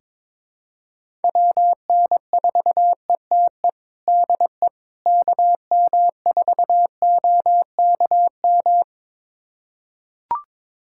Morse code, heard as a keyed tone, about 22 wpm, 700 Hz. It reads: WN4ETE DE KM4OKM
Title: USA Callsign DE Callsign 22wpm